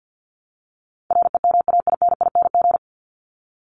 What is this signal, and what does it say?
Morse code, heard as a keyed tone, about 50 wpm, 700 Hz. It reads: REMAINING
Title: ING Suffix - Top 100 Words with Base Words 50wpm